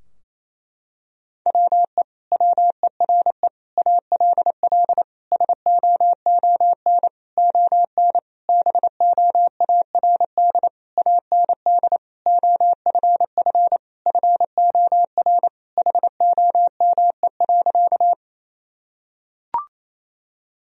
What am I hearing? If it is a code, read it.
Morse code, heard as a keyed tone, about 28 wpm, 700 Hz. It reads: WE WERE ALL SOON ON BOARD AND OFF FOR HOME.
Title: Sentences from Top 1000 Words - Review Part 4 - 28wpm